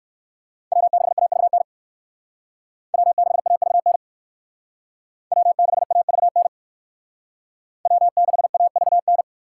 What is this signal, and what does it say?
Morse code, heard as a keyed tone, about 59 wpm, 700 Hz. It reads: W6AVN W6AVN W6AVN W6AVN